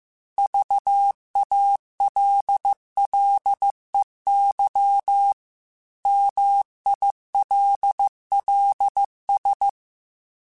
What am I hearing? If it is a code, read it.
Morse code, heard as a keyed tone, about 15 wpm, 800 Hz. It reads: VALLEY MILLS